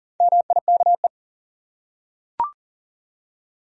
Morse code, heard as a keyed tone, about 40 wpm, 700 Hz. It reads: MIKE
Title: USA Common Names - Max 5 Letters - 40wpm